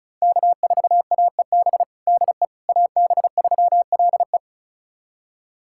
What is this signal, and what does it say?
Morse code, heard as a keyed tone, about 36 wpm, 700 Hz. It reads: K4AEB DE AB3LE
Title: USA Callsign DE Callsign - 5 Characters - 35wpm